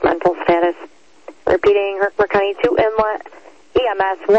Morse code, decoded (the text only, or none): none